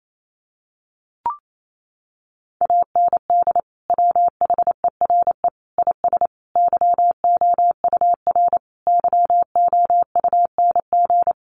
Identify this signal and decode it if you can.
Morse code, heard as a keyed tone, about 28 wpm, 700 Hz. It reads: AND WHERE IS YOUR YOUNG
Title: Sentences from Top 1300 Words 28wpm